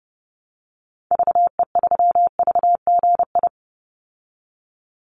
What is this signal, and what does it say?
Morse code, heard as a keyed tone, about 30 wpm, 700 Hz. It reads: VE3VGI
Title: Call Signs Repeated 2x - 30wpm